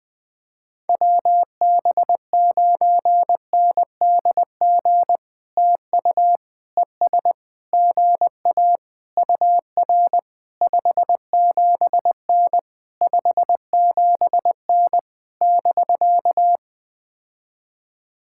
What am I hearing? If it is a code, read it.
Morse code, heard as a keyed tone, about 20 wpm, 700 Hz. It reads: WB9NDG TU ES GA UR 57N 57N <BK>
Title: POTA QSO Practice - 20wpm